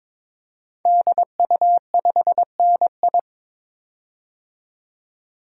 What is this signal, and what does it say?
Morse code, heard as a keyed tone, about 22 wpm, 700 Hz. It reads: DU5NI
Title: DXCC-100 Call Signs - 5 Characters - 22wpm